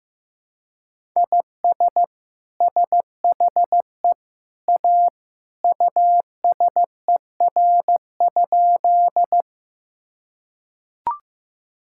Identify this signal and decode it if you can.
Morse code, heard as a keyed tone, about 15 wpm, 700 Hz. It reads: IS SHE A USER?
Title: Sentences from Top 1000 Words - Review Part 5 - 15wpm